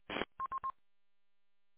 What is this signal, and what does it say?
There is no code.